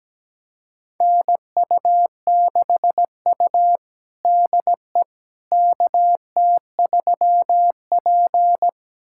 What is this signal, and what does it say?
Morse code, heard as a keyed tone, about 17 wpm, 700 Hz. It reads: NU6U DE KT3P
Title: USA Callsign DE Callsign - 4 Characters - 17wpm